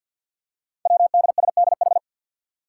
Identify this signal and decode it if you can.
Morse code, heard as a keyed tone, about 50 wpm, 700 Hz. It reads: WDSDH
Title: Five Challenging Letter-Number Combinations 50wpm